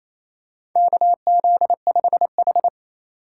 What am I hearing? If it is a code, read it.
Morse code, heard as a keyed tone, about 28 wpm, 700 Hz. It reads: KZ5H
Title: USA Call Signs - 4 Characters - 2x Character Spacing - 28wpm